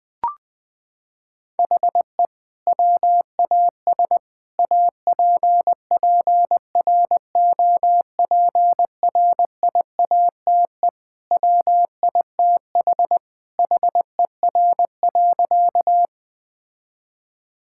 Morse code, heard as a keyed tone, about 21 wpm, 700 Hz. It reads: HE WAS APPROPRIATE WITH HER.
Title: Sentences from Top 1000 Words - Review Part 5 - 20wpm